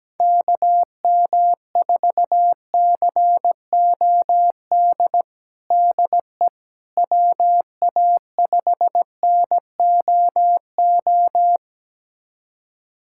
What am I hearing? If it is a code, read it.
Morse code, heard as a keyed tone, about 17 wpm, 700 Hz. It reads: KM4COD DE WA5NOO
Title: USA Callsign DE Callsign - 6 Characters - 17wpm